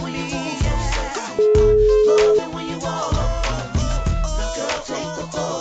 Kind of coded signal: telephone call progress tone

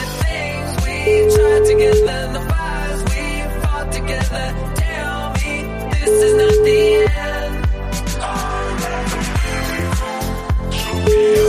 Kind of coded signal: telephone call progress tone

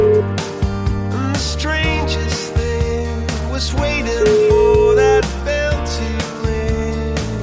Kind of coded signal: telephone call progress tone

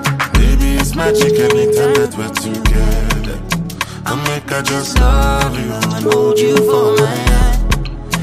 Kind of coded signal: telephone call progress tone

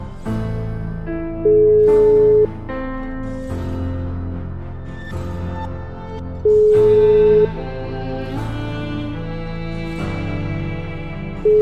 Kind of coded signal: telephone call progress tone

